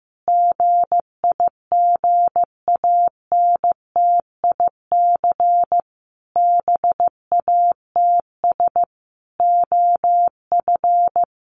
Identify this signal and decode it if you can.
Morse code, heard as a keyed tone, about 15 wpm, 700 Hz. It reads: GIGANTIC BATS OF
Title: A Sound Of Thunder 15wpm